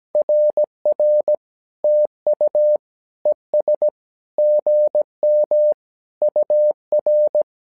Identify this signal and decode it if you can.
Morse code, heard as a keyed tone, about 17 wpm, 600 Hz. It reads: RR TU ES GM UR
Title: SOTA QSO Practice 17wpm